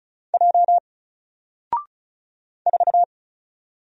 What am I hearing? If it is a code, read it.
Morse code, heard as a keyed tone, about 36 wpm, 700 Hz. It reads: J 4
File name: Single-Letter-Number-35wpm